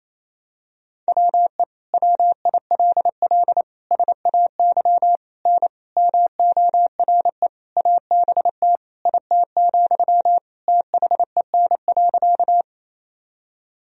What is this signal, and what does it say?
Morse code, heard as a keyed tone, about 28 wpm, 700 Hz. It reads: WE WILL SAY N MORE ABT IT, THEN.
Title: Sentences from Top 200 Words and Abbreviations 28wpm